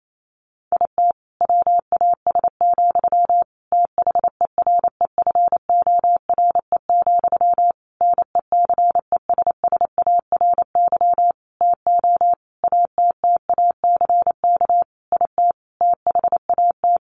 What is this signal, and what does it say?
Morse code, heard as a keyed tone, about 28 wpm, 700 Hz. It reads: IT WAS, THEREFORE, NECESSARY TO ATTACK IT THAT
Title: Sentences from Top 1500 Words 28wpm